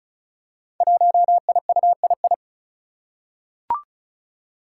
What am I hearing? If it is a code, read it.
Morse code, heard as a keyed tone, about 36 wpm, 700 Hz. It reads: 1IUII
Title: Five Challenging Letter-Number Combinations 35wpm